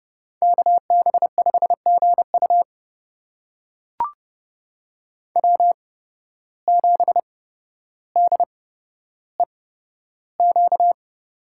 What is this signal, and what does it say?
Morse code, heard as a keyed tone, about 30 wpm, spaced as for 13 wpm, 700 Hz. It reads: KB5GU W7DEQ
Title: USA Call Signs - 5 Characters - 8x Character Spacing - 30wpm